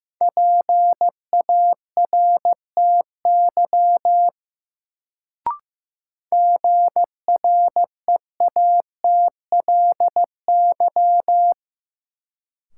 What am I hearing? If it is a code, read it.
Morse code, heard as a keyed tone, about 15 wpm, 700 Hz. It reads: PARTY GREATLY
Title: Sentences from Top 1400 Words - New Word Drill - 15wpm